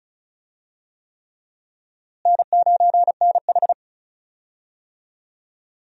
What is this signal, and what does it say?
Morse code, heard as a keyed tone, about 36 wpm, 700 Hz. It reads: N9NH
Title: USA Call Signs - 4 Characters - Repeated 3x - 35wpm